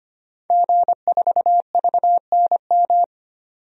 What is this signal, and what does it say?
Morse code, heard as a keyed tone, about 25 wpm, 700 Hz. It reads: G4VNM
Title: DXCC-100 Call Signs - 5 Characters - 25wpm